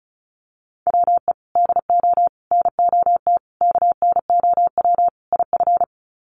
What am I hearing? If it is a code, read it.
Morse code, heard as a keyed tone, about 34 wpm, 700 Hz. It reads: WE DO NOT KNOW IF
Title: Sets of 5 Words Encore2 35wpm